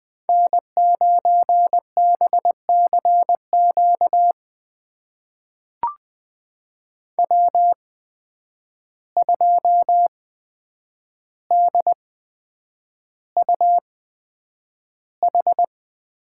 Morse code, heard as a keyed tone, about 20 wpm, spaced as for 9 wpm, 700 Hz. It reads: N9BCQ W2DUH
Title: USA Call Signs - 5 Characters - 8x Character Spacing - 20wpm